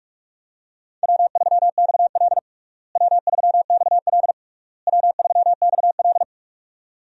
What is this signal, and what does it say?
Morse code, heard as a keyed tone, about 46 wpm, 700 Hz. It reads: W3XL W3XL W3XL
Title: USA Call Signs - 4 Characters - Repeated 3x - 45wpm